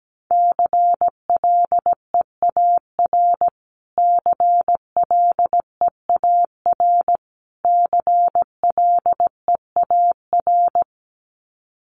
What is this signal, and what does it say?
Morse code, heard as a keyed tone, about 17 wpm, 700 Hz. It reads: CLEAR CLEAR CLEAR
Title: Top 500 Words Repeated 17wpm